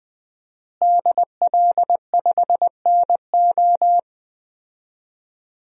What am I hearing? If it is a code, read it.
Morse code, heard as a keyed tone, about 20 wpm, 700 Hz. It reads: DL5NO
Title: DXCC-100 Call Signs - 5 Characters - Repeated 3x - 20wpm